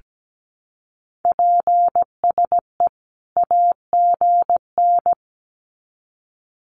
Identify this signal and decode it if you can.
Morse code, heard as a keyed tone, about 17 wpm, 700 Hz. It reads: PSE AGN
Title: Common QSO Phrases 17wpm